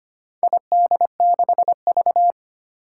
Morse code, heard as a keyed tone, about 25 wpm, 700 Hz. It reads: ID6V
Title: Four Challenging Letter-Number Combinations 25wpm